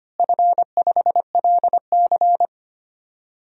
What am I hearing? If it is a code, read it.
Morse code, heard as a keyed tone, about 25 wpm, 700 Hz. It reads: F5LC
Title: DXCC-100 Call Signs - 4 Characters - Repeated 3x - 25wpm